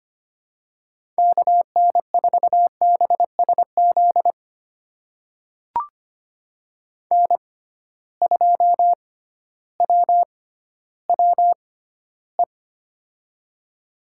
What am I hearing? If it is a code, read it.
Morse code, heard as a keyed tone, about 25 wpm, spaced as for 13 wpm, 700 Hz. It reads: KN4BSZ N2WWE
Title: USA Call Signs - 6x Character Spacing - 25wpm